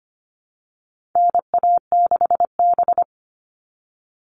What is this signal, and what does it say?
Morse code, heard as a keyed tone, about 25 wpm, 700 Hz. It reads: NA6B
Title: All Cut Numbers - 4-Digit 25wpm